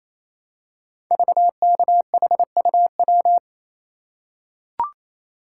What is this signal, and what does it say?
Morse code, heard as a keyed tone, about 27 wpm, 700 Hz. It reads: VKHUW
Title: Code Groups - 5 Letters - 28wpm